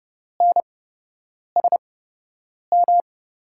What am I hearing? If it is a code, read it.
Morse code, heard as a keyed tone, about 30 wpm, spaced as for 8 wpm, 700 Hz. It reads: NSM
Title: USA Call Signs - 6 Characters - 8x Character Spacing - 30wpm